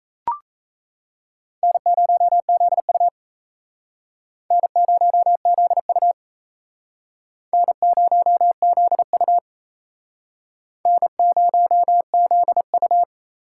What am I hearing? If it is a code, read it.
Morse code, heard as a keyed tone, about 37 wpm, 700 Hz. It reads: N0ZU N0ZU N0ZU N0ZU